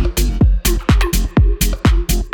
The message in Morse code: none